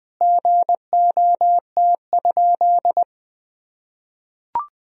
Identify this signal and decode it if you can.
Morse code, heard as a keyed tone, about 21 wpm, 700 Hz. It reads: GOT?
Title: Sentences from Top 300 Words and Abbreviations 20wpm